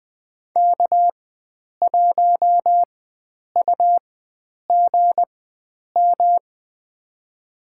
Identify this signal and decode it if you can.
Morse code, heard as a keyed tone, about 21 wpm, spaced as for 9 wpm, 700 Hz. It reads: K1UGM
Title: USA Call Signs - 5 Characters - 4x Character Spacing - 20wpm